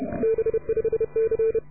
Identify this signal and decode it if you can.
Morse code, heard as a keyed tone, about 28 wpm, 450 Hz. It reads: B5C